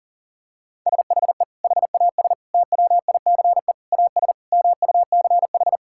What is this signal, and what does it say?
Morse code, heard as a keyed tone, about 40 wpm, 700 Hz. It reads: SHE HAS TWICE AS MUCH
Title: Sentences from Top 1600 Words 40wpm